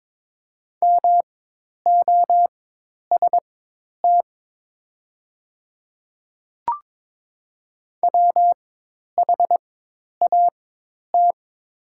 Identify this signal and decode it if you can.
Morse code, heard as a keyed tone, about 22 wpm, spaced as for 10 wpm, 700 Hz. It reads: MOST WHAT
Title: 4x Character Spacing - Top 200 Words - 22wpm